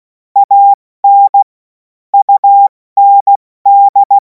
Morse code, heard as a keyed tone, about 16 wpm, 800 Hz. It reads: AN UND
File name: Morsekurs / Witze / Witz 03. März 2026 70 F80